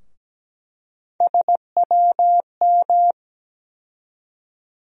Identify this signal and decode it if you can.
Morse code, heard as a keyed tone, about 17 wpm, 700 Hz. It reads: SWM
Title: USA Call Sign Postfixes - 3 Letters 17wpm